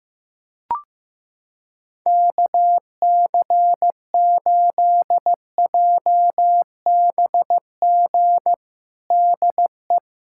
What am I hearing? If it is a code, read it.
Morse code, heard as a keyed tone, about 16 wpm, 700 Hz. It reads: KC8JBG DE
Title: USA Callsign DE Callsign - 6 Characters - 15wpm